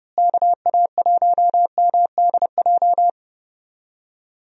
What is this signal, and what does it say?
Morse code, heard as a keyed tone, about 30 wpm, 700 Hz. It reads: KA1MDJ
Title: USA Call Signs - 6 Characters - Repeated 3x - 30wpm